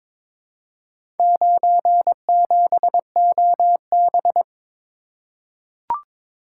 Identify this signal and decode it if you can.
Morse code, heard as a keyed tone, about 22 wpm, 700 Hz. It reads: 97OB
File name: practice / 22wpm / Four Letter-Number Combinations - 22wpm